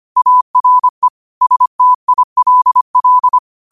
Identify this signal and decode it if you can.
Morse code, heard as a keyed tone, about 25 wpm, 1000 Hz. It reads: ARE STILL